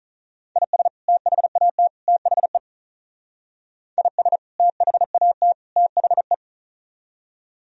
Morse code, heard as a keyed tone, about 46 wpm, 700 Hz. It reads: IS THAT THE IS THAT THE